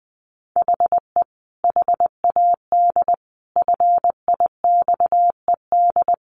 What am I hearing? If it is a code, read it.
Morse code, heard as a keyed tone, about 20 wpm, 700 Hz. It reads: HE HAD FIXED
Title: Sentences from Top 1500 Words 20wpm